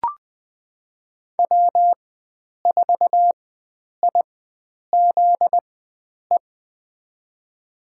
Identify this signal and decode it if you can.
Morse code, heard as a keyed tone, about 20 wpm, spaced as for 9 wpm, 700 Hz. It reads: W4IZE
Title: USA Call Signs - 5 Characters - 4x Character Spacing - 20wpm